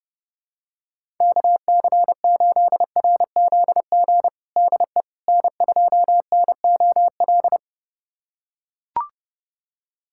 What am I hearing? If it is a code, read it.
Morse code, heard as a keyed tone, about 32 wpm, 700 Hz. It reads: KC8RZG DE N2NOL
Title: USA Callsign DE Callsign 30wpm